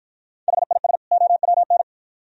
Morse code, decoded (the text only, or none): HIS OWN